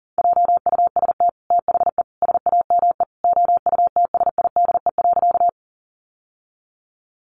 Morse code, heard as a keyed tone, about 40 wpm, 700 Hz. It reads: JUST THE SAME OUTSIDE.
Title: Sentences from Top 900 Words 40wpm